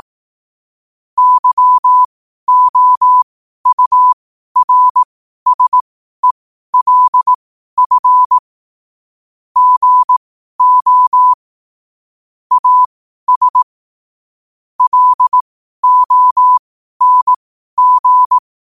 Morse code, heard as a keyed tone, about 18 wpm, spaced as for 12 wpm, 1000 Hz. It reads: YOURSELF GO AS LONG
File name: qotdfiles / 12wpm